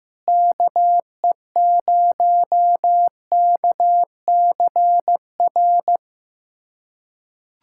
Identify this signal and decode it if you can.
Morse code, heard as a keyed tone, about 16 wpm, 700 Hz. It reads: KE0KCR
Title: USA Callsign DE Callsign - 6 Characters - 15wpm